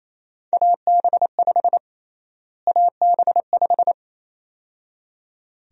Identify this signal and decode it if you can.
Morse code, heard as a keyed tone, about 27 wpm, 700 Hz. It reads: AB5 AB5
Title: USA Call Sign Prefixes - Repeated 2x - 28wpm